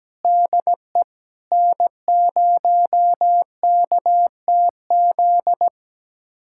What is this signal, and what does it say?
Morse code, heard as a keyed tone, about 17 wpm, 700 Hz. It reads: DE N0KTZ
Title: USA Callsign DE Callsign - 5 Characters - 17wpm